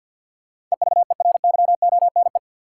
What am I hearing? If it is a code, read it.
Morse code, heard as a keyed tone, about 53 wpm, 700 Hz. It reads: EVERYONE
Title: Sentences from Top 800 Words - New Word Drill - 50wpm